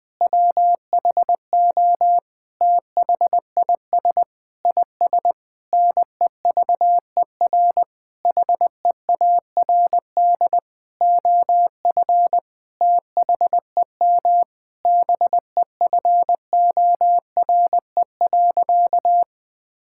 Morse code, heard as a keyed tone, about 20 wpm, 700 Hz. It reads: WHO THIS IS NEVER HEARD OF THEM BEFORE.
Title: Sentences with 9-Word Phrases - 20wpm